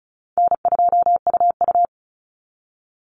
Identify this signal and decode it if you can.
Morse code, heard as a keyed tone, about 34 wpm, 700 Hz. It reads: N2UU
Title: Easy Call Signs Repeated 2x - 35wpm